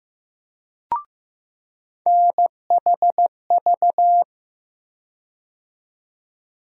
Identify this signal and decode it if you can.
Morse code, heard as a keyed tone, about 15 wpm, 700 Hz. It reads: NHV
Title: USA Call Sign Postfixes - 3 Letters 15wpm